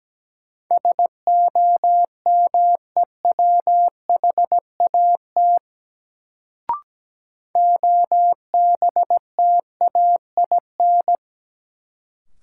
Morse code, heard as a keyed tone, about 17 wpm, 700 Hz. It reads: SOMEWHAT OBTAIN
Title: Sentences from Top 1300 Words - New Word Drill Alt - 17wpm